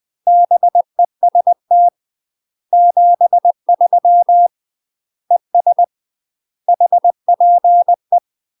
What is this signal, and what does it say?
Morse code, heard as a keyed tone, about 20 wpm, 700 Hz. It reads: BEST 73 ES HPE
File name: UK MASTER QSOs TRAINING / 20WPM / UK MASTER QSOs 0004